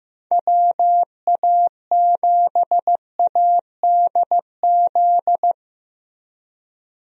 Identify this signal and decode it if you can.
Morse code, heard as a keyed tone, about 15 wpm, 700 Hz. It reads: WA7ADZ
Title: USA Call Signs - 6 Characters - Repeated 2x - 15wpm